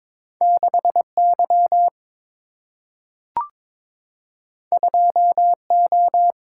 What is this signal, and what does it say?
Morse code, heard as a keyed tone, about 22 wpm, 700 Hz. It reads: 6 Y 2 O
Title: Single-Letter Single-Number 22wpm